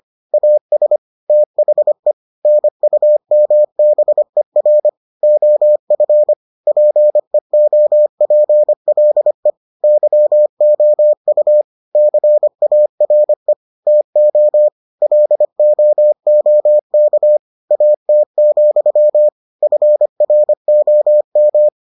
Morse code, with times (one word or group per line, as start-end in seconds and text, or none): 0.3-1.0 AS
1.3-2.1 THE
2.4-4.9 NUMBER
5.2-6.3 OF
6.7-9.5 PEOPLE
9.8-11.6 YOU
12.0-13.5 CARE
13.9-14.7 TO
15.0-17.4 LOOK
17.7-19.3 AT,
19.6-21.7 FROM